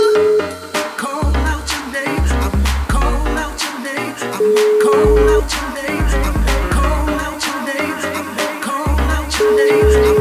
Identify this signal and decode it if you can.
Ringback tone (Europe, 425 Hz), with noise.